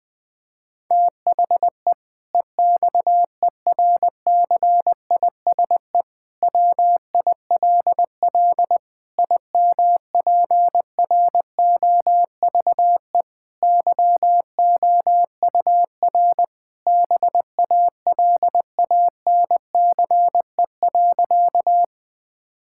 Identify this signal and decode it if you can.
Morse code, heard as a keyed tone, about 20 wpm, 700 Hz. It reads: THE EXERCISE WILL IMPROVE YOUR BALANCE.
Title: Sentences from Top 1000 Words - Review Part 6 - 20wpm